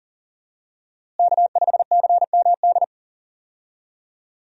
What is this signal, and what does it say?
Morse code, heard as a keyed tone, about 40 wpm, 700 Hz. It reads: K5CMD
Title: USA Call Signs - 5 Characters - Repeated 2x - 40wpm